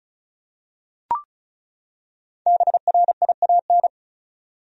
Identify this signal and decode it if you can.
Morse code, heard as a keyed tone, about 36 wpm, 700 Hz. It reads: BRIAN
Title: USA Common Names - Max 5 Letters - 35wpm